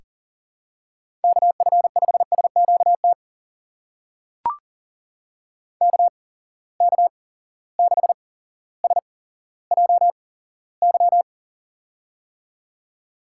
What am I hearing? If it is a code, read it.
Morse code, heard as a keyed tone, about 40 wpm, spaced as for 17 wpm, 700 Hz. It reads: KF5SQT KK6SJY
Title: USA Call Signs - 6 Characters - 8x Character Spacing - 40wpm